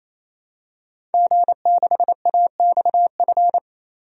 Morse code, heard as a keyed tone, about 28 wpm, 700 Hz. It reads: G6AXF
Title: DXCC-100 Call Signs - 5 Characters - 28wpm